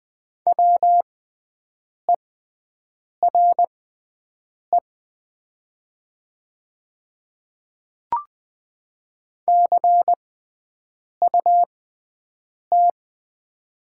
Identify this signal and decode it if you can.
Morse code, heard as a keyed tone, about 20 wpm, spaced as for 7 wpm, 700 Hz. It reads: WERE CUT
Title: 6x Character Spacing - Top 300 Words - 20wpm